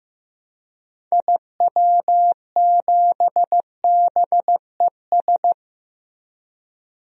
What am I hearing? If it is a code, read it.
Morse code, heard as a keyed tone, about 15 wpm, 700 Hz. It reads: IW7BES
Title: DXCC-100 Call Signs - 6 Characters - Repeated 3x - 15wpm